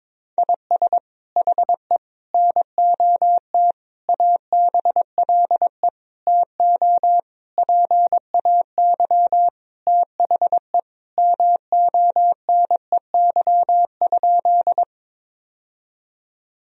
Morse code, heard as a keyed tone, about 22 wpm, 700 Hz. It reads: IS HE NOT ABLE TO PAY THE MONEY?
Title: Sentences from Top 1000 Words - Review Part 2 - 22wpm